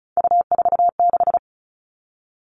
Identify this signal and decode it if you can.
Morse code, heard as a keyed tone, about 34 wpm, 700 Hz. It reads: U46
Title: All Cut Numbers - 3-Digit 35wpm